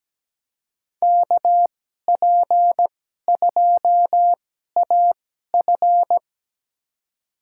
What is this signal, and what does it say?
Morse code, heard as a keyed tone, about 17 wpm, spaced as for 12 wpm, 700 Hz. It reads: KP2AF
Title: USA Call Signs - 5 Characters - 2x Character Spacing - 17wpm